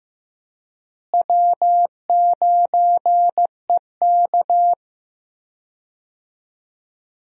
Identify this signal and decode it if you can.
Morse code, heard as a keyed tone, about 16 wpm, 700 Hz. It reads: W9EK
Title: USA Call Signs - 4 Characters - Repeated 2x - 15wpm